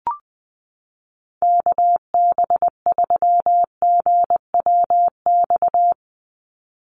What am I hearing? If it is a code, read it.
Morse code, heard as a keyed tone, about 20 wpm, 700 Hz. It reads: KB3GWX
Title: USA Call Signs 20wpm